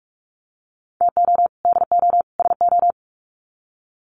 Morse code, heard as a keyed tone, about 44 wpm, 700 Hz. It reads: TO DO SO
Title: Sets of 3 Words 45wpm